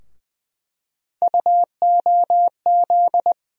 Morse code, heard as a keyed tone, about 21 wpm, 700 Hz. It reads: UOZ
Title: USA Call Sign Postfixes - 3 Letters 20wpm